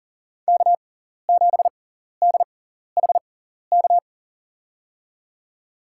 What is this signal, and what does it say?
Morse code, heard as a keyed tone, about 40 wpm, spaced as for 14 wpm, 700 Hz. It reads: K7DHK
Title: USA Call Signs - 5 Characters - 6x Character Spacing - 40wpm